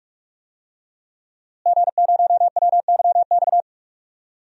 Morse code, heard as a keyed tone, about 44 wpm, 700 Hz. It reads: G0WYX
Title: DXCC-100 Call Signs - 5 Characters - Repeated 2x - 45wpm